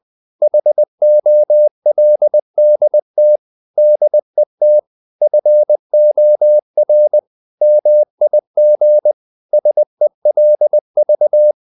Morse code, heard as a keyed tone, about 20 wpm, 600 Hz. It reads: HOLDT DET FOR MIG SELV
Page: Ferie 20 wpm